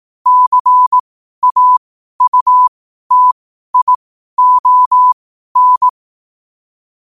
Morse code, heard as a keyed tone, about 18 wpm, spaced as for 12 wpm, 1000 Hz. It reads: CAUTION
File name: qotdfiles / 12wpm